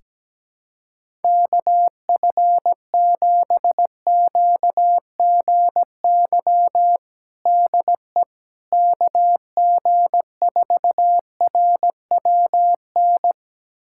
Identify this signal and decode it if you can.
Morse code, heard as a keyed tone, about 17 wpm, 700 Hz. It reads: KF7QGY DE KG4RWN
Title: USA Callsign DE Callsign - 6 Characters - 17wpm